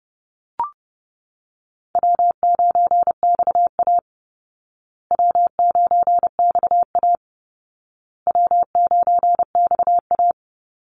Morse code, heard as a keyed tone, about 30 wpm, 700 Hz. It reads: W9XA W9XA W9XA
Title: Call Signs Repeated 2x - 30wpm